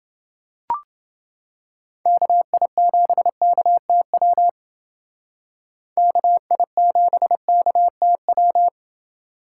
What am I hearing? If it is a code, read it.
Morse code, heard as a keyed tone, about 28 wpm, 700 Hz. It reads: KI7KTW KI7KTW